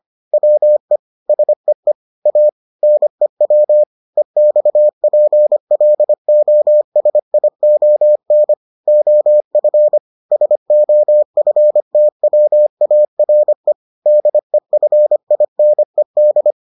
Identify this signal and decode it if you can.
Morse code, heard as a keyed tone, about 25 wpm, 600 Hz. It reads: WE SEE A NEW EXPLOSION OF SOFTWARE DEFINED